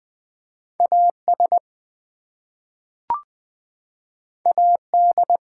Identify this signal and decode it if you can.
Morse code, heard as a keyed tone, about 20 wpm, 700 Hz. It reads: AS AD
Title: Two Letter Words - 20wpm